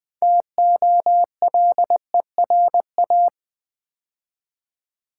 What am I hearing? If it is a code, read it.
Morse code, heard as a keyed tone, about 20 wpm, 700 Hz. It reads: TOLERA
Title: Build A Word - 3 Letters At A Time - 9-Letter Words - 20wpm